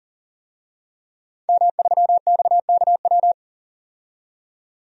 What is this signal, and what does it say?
Morse code, heard as a keyed tone, about 40 wpm, 700 Hz. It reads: M3XKW
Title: DXCC-100 Call Signs - 5 Characters - 40wpm